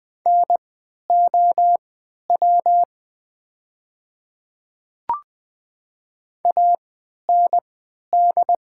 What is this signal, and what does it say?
Morse code, heard as a keyed tone, about 20 wpm, spaced as for 11 wpm, 700 Hz. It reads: NOW AND
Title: 3x Character Spacing - Top 100 Words - 20wpm